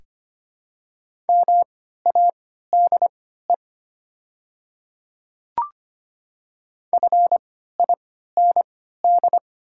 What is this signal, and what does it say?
Morse code, heard as a keyed tone, about 25 wpm, spaced as for 14 wpm, 700 Hz. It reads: MADE FIND
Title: 3x Character Spacing - Top 200 Words - 25wpm